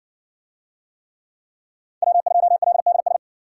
Morse code, heard as a keyed tone, about 60 wpm, 700 Hz. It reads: W2LDS